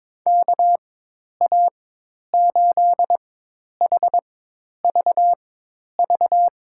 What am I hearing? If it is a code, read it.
Morse code, heard as a keyed tone, about 21 wpm, spaced as for 10 wpm, 700 Hz. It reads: KA8HVV
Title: USA Call Signs - 6 Characters - 4x Character Spacing - 22wpm